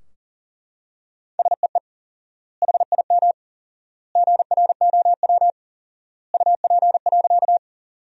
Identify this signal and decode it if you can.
Morse code, heard as a keyed tone, about 40 wpm, 700 Hz. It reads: SEE HIM GROW UP.
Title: 4x Word-Spacing - Sentences from Top 300 Words - 40wpm